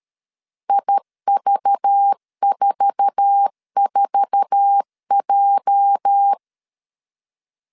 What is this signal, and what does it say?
Morse code, heard as a keyed tone, about 13 wpm, 800 Hz. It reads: IV44J
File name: LEZIONI / 8 YJ45 / 8 YJ4 12wpm 2151